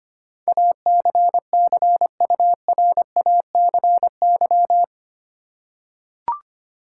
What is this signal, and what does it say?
Morse code, heard as a keyed tone, about 25 wpm, 700 Hz. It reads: ACCURACY
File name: Eight Letter Words - 25wpm